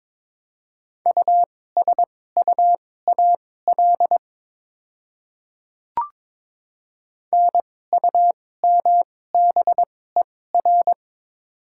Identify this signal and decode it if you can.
Morse code, heard as a keyed tone, about 22 wpm, spaced as for 16 wpm, 700 Hz. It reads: USUAL NUMBER
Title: 2x Character Spacing - Top 500 Words - 22wpm